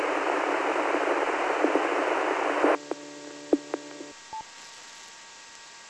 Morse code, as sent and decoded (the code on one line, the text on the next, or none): none